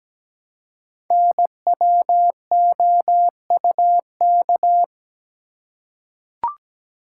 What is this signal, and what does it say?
Morse code, heard as a keyed tone, about 17 wpm, 700 Hz. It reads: NWOUK
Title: Code Groups - 5 Letters - 17wpm